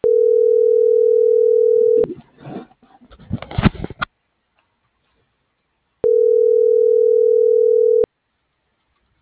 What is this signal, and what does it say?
Ringback tone (North America), 440 + 480 Hz.